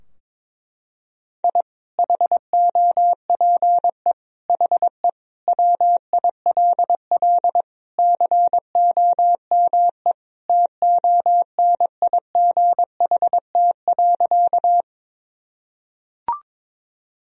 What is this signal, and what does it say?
Morse code, heard as a keyed tone, about 21 wpm, 700 Hz. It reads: I HOPE HE WILL COME TONIGHT.
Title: Sentences from Top 1000 Words - Review Part 3 - 22wpm